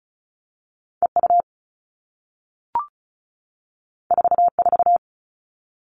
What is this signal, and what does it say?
Morse code, heard as a keyed tone, about 36 wpm, 700 Hz. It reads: EU 44
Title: All Cut Numbers - 2-Digit 35wpm